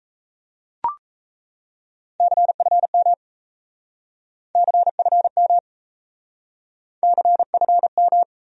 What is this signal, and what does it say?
Morse code, heard as a keyed tone, about 36 wpm, 700 Hz. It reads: CFM CFM CFM